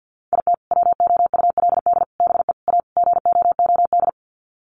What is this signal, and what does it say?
Morse code, heard as a keyed tone, about 50 wpm, 700 Hz. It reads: IT WOULD BE A GOOD
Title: Sets of 5 Words Encore2 50wpm